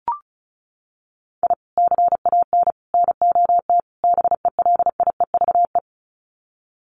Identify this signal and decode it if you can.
Morse code, heard as a keyed tone, about 34 wpm, 700 Hz. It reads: I CAN NOT BELIEVE
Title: Sets of 4 Words 35wpm